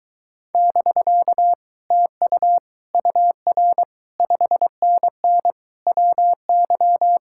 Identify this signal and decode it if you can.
Morse code, heard as a keyed tone, about 22 wpm, 700 Hz. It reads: <BK> TU UR 5NN WY